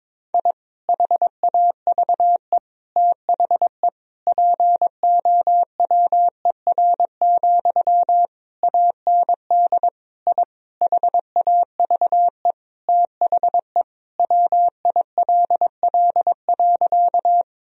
Morse code, heard as a keyed tone, about 22 wpm, 700 Hz. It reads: I HAVE THE POWER, AND I HAVE THE WILL.
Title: Sentences from Top 1000 Words - Review Part 2 - 22wpm